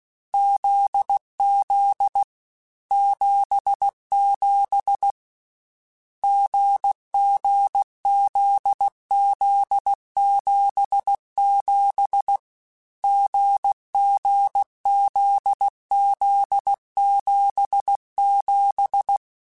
Morse code, heard as a keyed tone, about 16 wpm, 800 Hz. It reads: ZZ 77 GG ZZ 77 GG ZZ 77
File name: Difficult Combos / Difficult Combos-mp3s / GZ GZ7 @15